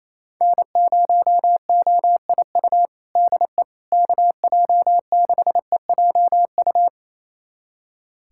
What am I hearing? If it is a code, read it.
Morse code, heard as a keyed tone, about 27 wpm, 700 Hz. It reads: N0OIU DE KJ6EJU
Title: USA Callsign DE Callsign 28wpm